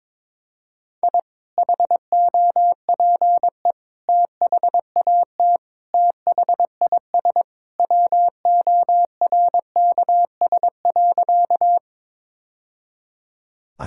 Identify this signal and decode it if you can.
Morse code, heard as a keyed tone, about 23 wpm, 700 Hz. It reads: I HOPE THAT THIS WORKS.
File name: practice / 22wpm / Sentences with 2-Word Phrases - 22wpm